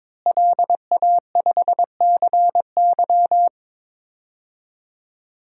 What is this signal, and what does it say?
Morse code, heard as a keyed tone, about 22 wpm, 700 Hz. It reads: LA5CY
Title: DXCC-100 Call Signs - 5 Characters - Repeated 3x - 22wpm